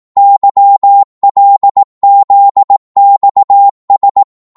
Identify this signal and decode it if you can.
Morse code, heard as a keyed tone, about 18 wpm, 800 Hz. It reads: YLZXS